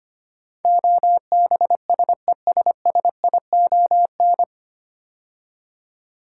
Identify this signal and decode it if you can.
Morse code, heard as a keyed tone, about 25 wpm, 700 Hz. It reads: OBSESSION
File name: practice / 25wpm / 9 Letter Words - First 3 Letters Given - 25wpm